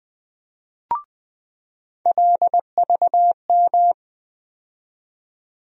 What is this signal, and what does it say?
Morse code, heard as a keyed tone, about 20 wpm, 700 Hz. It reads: LVM
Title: USA Call Sign Postfixes - 3 Letters 20wpm